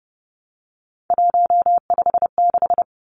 Morse code, heard as a keyed tone, about 30 wpm, 700 Hz. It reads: 156
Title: Numbers - 3-Digit 30wpm